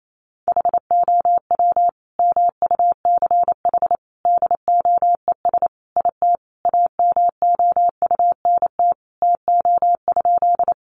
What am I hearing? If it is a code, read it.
Morse code, heard as a keyed tone, about 28 wpm, 700 Hz. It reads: HOW MUCH DOES IT AMOUNT TO?
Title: Sentences from Top 700 Words 28wpm